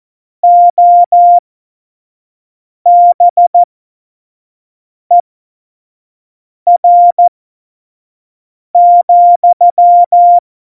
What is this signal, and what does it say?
Morse code, heard as a keyed tone, about 14 wpm, spaced as for 5 wpm, 700 Hz. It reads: OBER,
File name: Morsekurs / Witze / Witz 12. November 2019, 25 F70